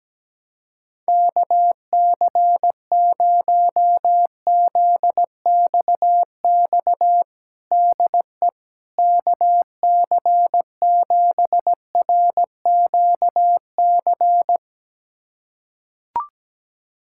Morse code, heard as a keyed tone, about 17 wpm, 700 Hz. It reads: KC0ZXX DE KC7RQC
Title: USA Callsign DE Callsign 17wpm